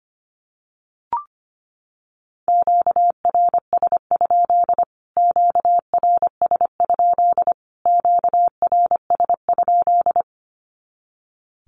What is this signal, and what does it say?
Morse code, heard as a keyed tone, about 25 wpm, 700 Hz. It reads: QRS? QRS? QRS?
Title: Q-Codes Repeated 3x 25wpm